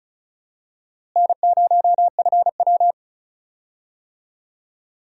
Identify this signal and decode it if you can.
Morse code, heard as a keyed tone, about 36 wpm, 700 Hz. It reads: N0FW
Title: USA Call Signs - 4 Characters - Repeated 2x - 35wpm